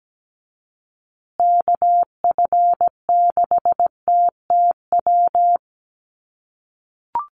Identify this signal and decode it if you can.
Morse code, heard as a keyed tone, about 17 wpm, 700 Hz. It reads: KF6TTW
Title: USA Call Signs Repeated 3x 17wpm